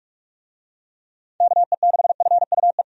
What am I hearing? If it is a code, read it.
Morse code, heard as a keyed tone, about 48 wpm, 700 Hz. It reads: KE6FUE
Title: USA Call Signs - 6 Characters - 45wpm